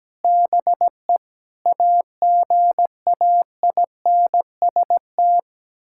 Morse code, heard as a keyed tone, about 17 wpm, 700 Hz. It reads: BE AGAINST
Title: Sentences from Top 1000 Words - Review Part 6 - 17wpm